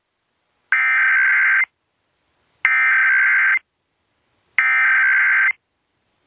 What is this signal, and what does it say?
Emergency Alert System (SAME) tones (header bursts), heard as harsh buzzing bursts. It is heard from 0.7 to 5.5 s.